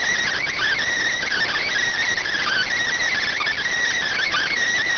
Emergency Alert System (SAME) tones (header bursts), heard as harsh buzzing bursts. It is heard from 0.3 to 4.5 s.